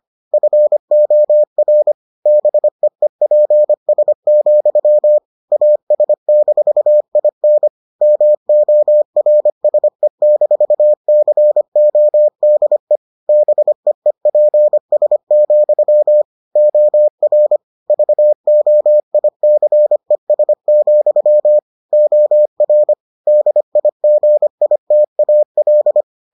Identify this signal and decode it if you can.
Morse code, heard as a keyed tone, about 25 wpm, 600 Hz. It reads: FOR BEEPS, AS-IN MORSE-CODE BEEPS, OR VOICES, OR DIGITAL